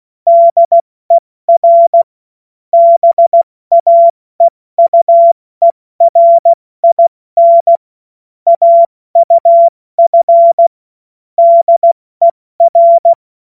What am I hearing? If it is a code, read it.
Morse code, heard as a keyed tone, about 16 wpm, 680 Hz. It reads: DER BAEUERIN AUF DER
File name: Morsekurs / Witze / Witz 13. April 2021, 70 F80